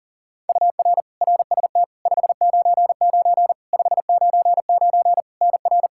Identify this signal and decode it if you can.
Morse code, heard as a keyed tone, about 40 wpm, 700 Hz. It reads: UR RST 599 599 NR
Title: Practice QSO - Exchange 1 - Beginning - 40wpm